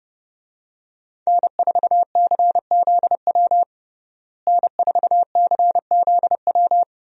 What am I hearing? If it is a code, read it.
Morse code, heard as a keyed tone, about 30 wpm, 700 Hz. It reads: N4CZW N4CZW
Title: USA Call Signs - 5 Characters - Repeated 2x - 30wpm